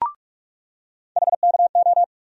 Morse code, heard as a keyed tone, about 44 wpm, 700 Hz. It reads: HKO
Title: USA Call Sign Postfixes - 3 Letters 45wpm